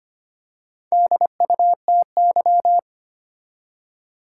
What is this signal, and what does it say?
Morse code, heard as a keyed tone, about 25 wpm, 700 Hz. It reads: DUTY
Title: Four Letter Words - 25wpm